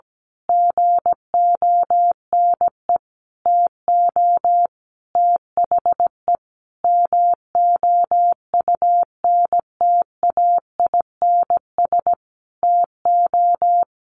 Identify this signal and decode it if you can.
Morse code, heard as a keyed tone, about 17 wpm, 700 Hz. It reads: GONE TO THE MOUNTAINS TO
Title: A Sound Of Thunder 17wpm